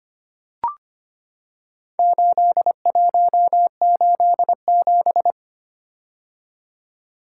Morse code, heard as a keyed tone, about 25 wpm, 700 Hz. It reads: 8187
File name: practice / 25wpm / Four Letter-Number Combinations - 25wpm